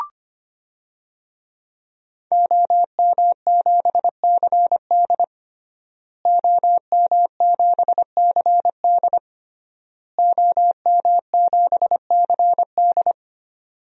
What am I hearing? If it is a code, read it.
Morse code, heard as a keyed tone, about 25 wpm, 700 Hz. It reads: OM7CD OM7CD OM7CD
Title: DXCC-100 Call Signs - 5 Characters - Repeated 3x - 25wpm